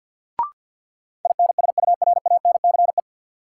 Morse code, heard as a keyed tone, about 50 wpm, 700 Hz. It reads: INSURANCE
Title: Sentences from Top 1100 Words - New Word Drill - 50wpm